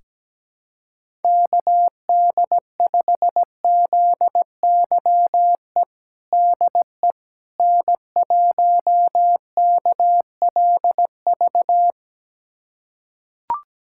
Morse code, heard as a keyed tone, about 17 wpm, 700 Hz. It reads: KD5ZYE DE N1KLV
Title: USA Callsign DE Callsign 17wpm